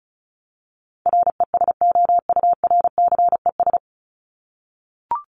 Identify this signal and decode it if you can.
Morse code, heard as a keyed tone, about 34 wpm, 700 Hz. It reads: RESOURCES
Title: ES:S Suffix - Top 100 Words with Base Words 35wpm